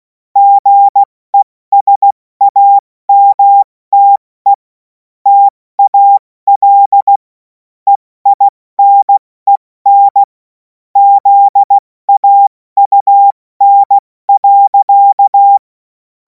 Morse code, heard as a keyed tone, about 16 wpm, 800 Hz. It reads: GESAMTE TAL EINEN ZAUN.
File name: Morsekurs / Witze / Witz 14. April 2026 70 F80